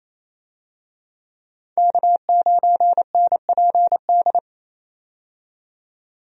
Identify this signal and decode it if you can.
Morse code, heard as a keyed tone, about 28 wpm, 700 Hz. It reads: K9NPD
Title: USA Call Signs - 5 Characters - 28wpm